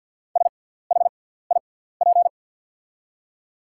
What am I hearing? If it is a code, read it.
Morse code, heard as a keyed tone, about 48 wpm, spaced as for 17 wpm, 700 Hz. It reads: SHIP
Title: 6x Character Spacing - Top 500 Words - 50wpm